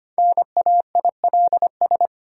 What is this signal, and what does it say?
Morse code, heard as a keyed tone, about 25 wpm, 700 Hz. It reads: NAILS
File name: practice / 25wpm / Difficult Words with Hints - 25wpm